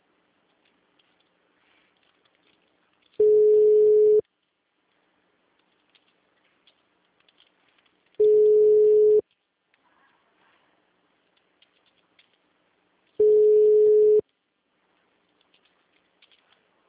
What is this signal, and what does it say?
Ringback tone (Europe, 425 Hz).
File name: out- -2008- - - .4809